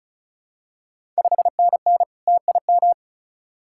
Morse code, heard as a keyed tone, about 36 wpm, 700 Hz. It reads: 5NN TIM
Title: Ontario QSO Party Exchange 35wpm